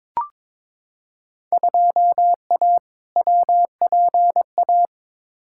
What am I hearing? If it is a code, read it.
Morse code, heard as a keyed tone, about 21 wpm, 700 Hz. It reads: 2A WPA
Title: Field Day Exchanges - 22wpm